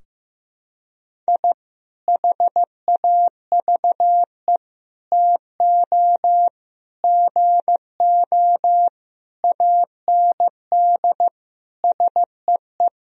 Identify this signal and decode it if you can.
Morse code, heard as a keyed tone, about 15 wpm, 700 Hz. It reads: I HAVE TO GO AND SEE
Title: Sentences from Top 1000 Words - Review Part 4 - 15wpm